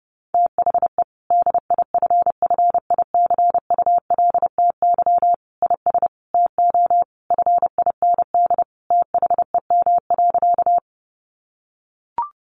Morse code, heard as a keyed tone, about 30 wpm, 700 Hz. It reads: THE DIFFICULTY IS TO FIND THEM.
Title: Sentences from Top 1500 Words 30wpm